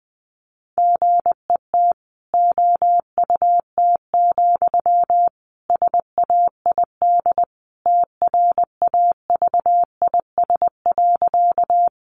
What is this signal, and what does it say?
Morse code, heard as a keyed tone, about 20 wpm, 700 Hz. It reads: GET OUT, SAID TRAVIS.
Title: A Sound Of Thunder 20wpm